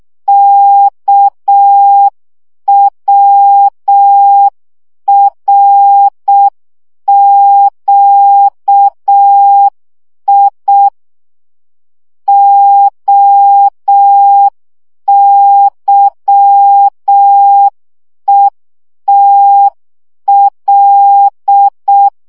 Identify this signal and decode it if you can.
Morse code, heard as a keyed tone, about 6 wpm, 800 Hz. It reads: KWRQI OYETL
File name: morze vizsga 2011-3 5